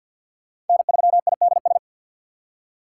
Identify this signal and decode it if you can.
Morse code, heard as a keyed tone, about 46 wpm, 700 Hz. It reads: N3IDS
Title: USA Call Signs - 5 Characters - 2x Character Spacing - 50wpm